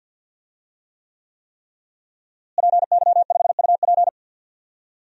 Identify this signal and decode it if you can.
Morse code, heard as a keyed tone, about 50 wpm, 700 Hz. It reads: PY5UP
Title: DXCC-100 Call Signs - 5 Characters - Repeated 3x - 50wpm